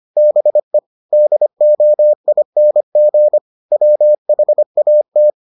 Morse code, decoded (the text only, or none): BE DOING WHAT